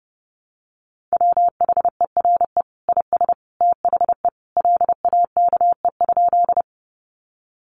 Morse code, heard as a keyed tone, about 30 wpm, 700 Hz. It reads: WHERE IS THE LAKE?
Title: Sentences from Top 1100 Words 30wpm